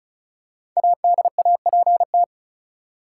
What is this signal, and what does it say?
Morse code, heard as a keyed tone, about 36 wpm, 700 Hz. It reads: ADAPT
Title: Five Letter Words - 35wpm